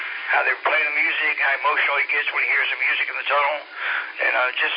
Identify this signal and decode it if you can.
Emergency Alert System (SAME) tones, heard as harsh buzzing bursts, with speech. It is heard from 1.2 to 4.1 s.